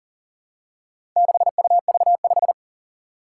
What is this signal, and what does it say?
Morse code, heard as a keyed tone, about 40 wpm, 700 Hz. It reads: 6UV5
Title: Four Challenging Letter-Number Combinations 40wpm